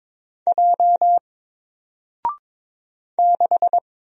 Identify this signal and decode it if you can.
Morse code, heard as a keyed tone, about 22 wpm, 700 Hz. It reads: J 6
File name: Single-Letter-Number-22wpm